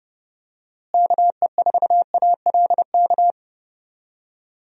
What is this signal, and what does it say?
Morse code, heard as a keyed tone, about 30 wpm, 700 Hz. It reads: KE4ALK
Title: USA Call Signs - 6 Characters - Repeated 3x - 30wpm